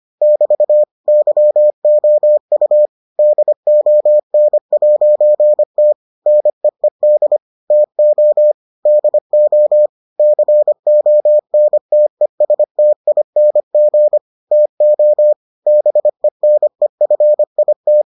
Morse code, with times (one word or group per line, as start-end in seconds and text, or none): 0.2-0.8 <BT>
1.1-2.9 YOU
3.2-5.9 DON'T
6.3-7.4 NEED
7.7-8.5 TO
8.9-9.9 DO
10.2-14.2 CONTESTING
14.5-15.3 TO
15.7-18.0 BENEFIT